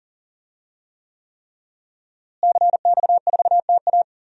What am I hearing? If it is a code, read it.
Morse code, heard as a keyed tone, about 40 wpm, 700 Hz. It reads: CX4TA
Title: DXCC-100 Call Signs - 5 Characters - Repeated 3x - 40wpm